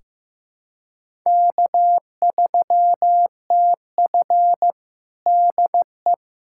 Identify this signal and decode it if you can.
Morse code, heard as a keyed tone, about 15 wpm, 700 Hz. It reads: K3TF DE
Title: Practice QSO - Exchange 4 - Ending - 15wpm